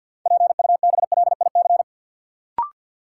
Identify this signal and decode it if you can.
Morse code, heard as a keyed tone, about 49 wpm, 700 Hz. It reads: PUBLIC
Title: Sentences from Top 600 Words - New Word Drill - 50wpm